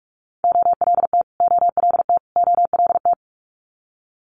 Morse code, heard as a keyed tone, about 46 wpm, 700 Hz. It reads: OLT OLT OLT
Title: Multi-letter Phonemes Repeated 3x 45wpm